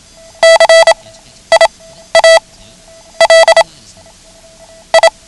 Morse code, heard as a keyed tone, about 27 wpm, spaced as for 11 wpm, 660 Hz. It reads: CIAL I
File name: ham radio / CodeQuick / Code Quick 09